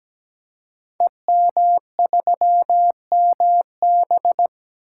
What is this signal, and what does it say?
Morse code, heard as a keyed tone, about 17 wpm, 700 Hz. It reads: EM3MB
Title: Code Groups - 5 Letter:Numbers - 17wpm